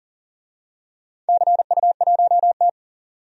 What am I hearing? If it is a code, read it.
Morse code, heard as a keyed tone, about 40 wpm, 700 Hz. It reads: CU1T
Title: DXCC-100 Call Signs - 4 Characters - 40wpm